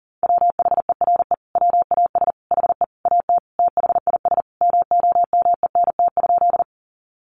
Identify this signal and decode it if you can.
Morse code, heard as a keyed tone, about 40 wpm, 700 Hz. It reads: WHERE WAS HE AT THIS MOMENT?
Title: Sentences from Top 1000 Words 40wpm